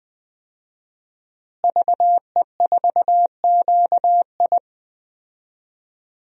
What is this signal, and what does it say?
Morse code, heard as a keyed tone, about 20 wpm, 700 Hz. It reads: VE4QI
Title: DXCC-100 Call Signs - 5 Characters - Repeated 3x - 20wpm